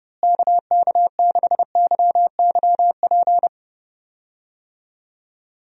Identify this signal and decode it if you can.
Morse code, heard as a keyed tone, about 32 wpm, 700 Hz. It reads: KK6YYP
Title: USA Call Signs - 6 Characters - Repeated 2x - 30wpm